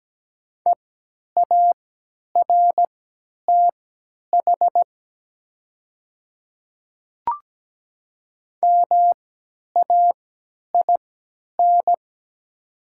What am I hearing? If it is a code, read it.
Morse code, heard as a keyed tone, about 17 wpm, spaced as for 10 wpm, 700 Hz. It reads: EARTH MAIN
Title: 3x Character Spacing - Top 300 Words - 17wpm